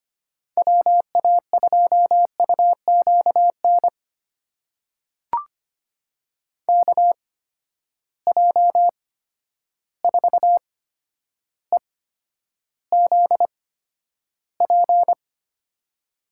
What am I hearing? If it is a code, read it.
Morse code, heard as a keyed tone, about 25 wpm, spaced as for 11 wpm, 700 Hz. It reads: WA2UQN KJ4EZP
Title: USA Call Signs - 8x Character Spacing - 25wpm